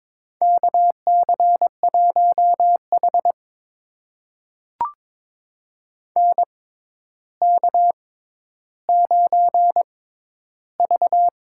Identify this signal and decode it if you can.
Morse code, heard as a keyed tone, about 22 wpm, spaced as for 11 wpm, 700 Hz. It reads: KC1H NK9V
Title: USA Call Signs - 4 Characters - 6x Character Spacing - 22wpm